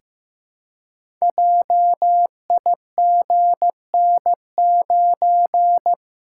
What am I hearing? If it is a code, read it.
Morse code, heard as a keyed tone, about 15 wpm, 700 Hz. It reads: JIGN9
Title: Code Groups - 5 Letter:Numbers - 15wpm